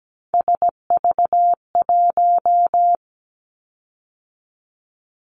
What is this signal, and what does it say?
Morse code, heard as a keyed tone, about 17 wpm, 700 Hz. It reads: SV1
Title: Callsign Prefixes for Top 50 DXCC - 17wpm